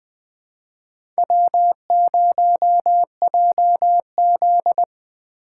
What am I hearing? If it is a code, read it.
Morse code, heard as a keyed tone, about 20 wpm, 700 Hz. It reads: W0JZ
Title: USA Call Signs - 4 Characters - 2x Character Spacing - 20wpm